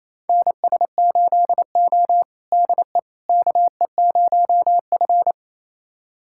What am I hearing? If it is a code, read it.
Morse code, heard as a keyed tone, about 28 wpm, 700 Hz. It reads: NS8O DE KE0F
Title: USA Callsign DE Callsign - 4 Characters - 28wpm